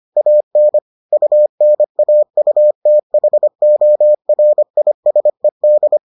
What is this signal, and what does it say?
Morse code, heard as a keyed tone, about 25 wpm, 600 Hz. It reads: AN UNAUTHORISED